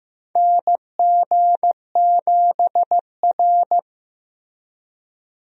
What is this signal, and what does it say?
Morse code, heard as a keyed tone, about 15 wpm, 700 Hz. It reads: NG7R
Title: USA Call Signs - 4 Characters - Repeated 2x - 15wpm